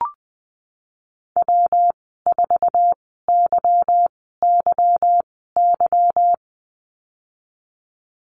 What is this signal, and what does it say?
Morse code, heard as a keyed tone, about 19 wpm, spaced as for 14 wpm, 700 Hz. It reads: W4YYY
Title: 161 Lesson 26g - Letter g - Call Signs 20wpm